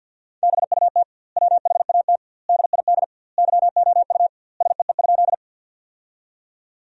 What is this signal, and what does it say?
Morse code, heard as a keyed tone, about 50 wpm, 700 Hz. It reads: BUT WHAT DID YOU SEE?
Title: 2x Word-Spacing - Sentences from Top 100 Words - 50wpm